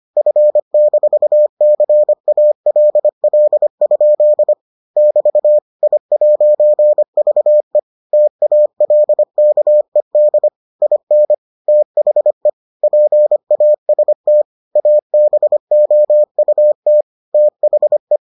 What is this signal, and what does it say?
Morse code, heard as a keyed tone, about 25 wpm, 600 Hz. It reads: F-CALL? <BT> I'VE TALKED IN THE PAST ABOUT THE